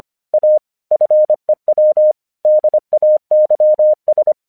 Morse code, heard as a keyed tone, about 25 wpm, 610 Hz. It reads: A FEW DAYS
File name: 1 English 25wpm / AQC17